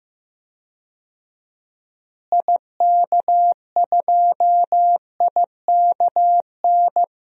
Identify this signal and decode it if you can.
Morse code, heard as a keyed tone, about 16 wpm, 700 Hz. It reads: IK2IKN
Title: DXCC-100 Call Signs - 6 Characters - Repeated 3x - 15wpm